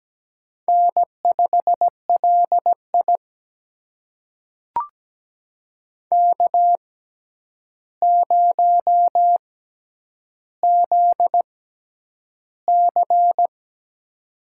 Morse code, heard as a keyed tone, about 17 wpm, spaced as for 9 wpm, 700 Hz. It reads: N5LI K0ZC
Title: USA Call Signs - 6x Character Spacing - 17wpm